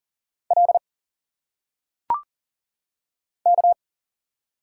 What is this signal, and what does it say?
Morse code, heard as a keyed tone, about 40 wpm, 700 Hz. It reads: L K
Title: Single Letters - Phonetic Pronunciation 40wpm